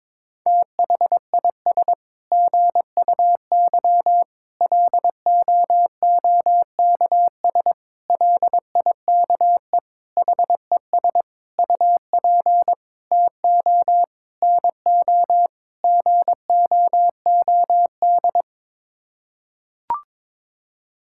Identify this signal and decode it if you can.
Morse code, heard as a keyed tone, about 22 wpm, 700 Hz. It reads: THIS GUY LOOKS LIKE HES UP TO NO GOOD
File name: Sets of 9 Words - 22wpm